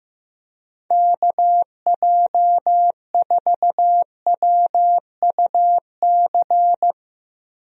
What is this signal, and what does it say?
Morse code, heard as a keyed tone, about 15 wpm, 700 Hz. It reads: KJ4WUC
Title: USA Call Signs - 6 Characters - 15wpm